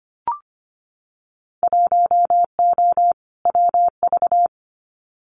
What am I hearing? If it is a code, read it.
Morse code, heard as a keyed tone, about 25 wpm, 700 Hz. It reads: 1O WV
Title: Winter Field Day Exchanges 25wpm